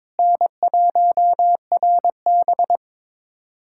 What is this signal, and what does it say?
Morse code, heard as a keyed tone, about 22 wpm, 700 Hz. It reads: N1RB
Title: USA Callsign DE Callsign - 4 Characters - 22wpm